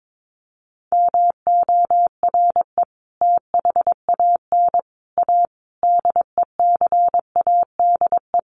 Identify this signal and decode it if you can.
Morse code, heard as a keyed tone, about 22 wpm, 700 Hz. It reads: MORE THAN A DECADE
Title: Sets of 5 Words Encore2 22wpm